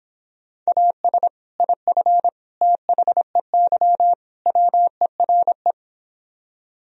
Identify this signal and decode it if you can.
Morse code, heard as a keyed tone, about 26 wpm, 700 Hz. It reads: AS IF THEY WERE